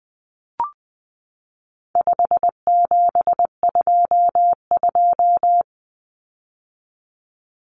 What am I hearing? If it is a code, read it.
Morse code, heard as a keyed tone, about 20 wpm, 700 Hz. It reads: 5722
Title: Common Cut Numbers - 4-Digit 20wpm